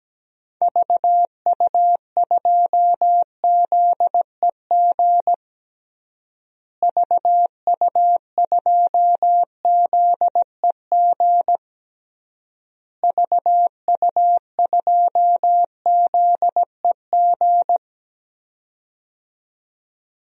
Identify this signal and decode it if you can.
Morse code, heard as a keyed tone, about 17 wpm, 700 Hz. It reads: VU2ZEG VU2ZEG VU2ZEG
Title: DXCC-100 Call Signs - 6 Characters - Repeated 3x - 17wpm